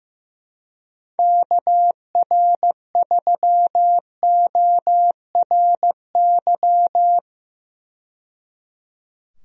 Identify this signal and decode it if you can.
Morse code, heard as a keyed tone, about 15 wpm, 700 Hz. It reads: KR3ORY
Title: USA Call Signs - 6 Characters - 15wpm